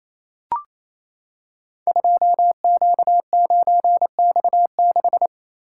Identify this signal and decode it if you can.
Morse code, heard as a keyed tone, about 28 wpm, 700 Hz. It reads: 2Q9X6
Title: Code Groups - 5 Letter:Numbers - 28wpm